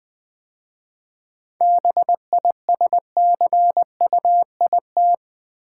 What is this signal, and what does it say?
Morse code, heard as a keyed tone, about 20 wpm, 700 Hz. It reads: BISCUIT
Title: 7 Letter Words - First 3 Letters Given - 20wpm